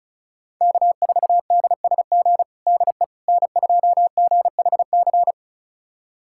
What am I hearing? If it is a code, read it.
Morse code, heard as a keyed tone, about 34 wpm, 700 Hz. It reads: K4DSG DE N2GHC
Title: USA Callsign DE Callsign - 5 Characters - 35wpm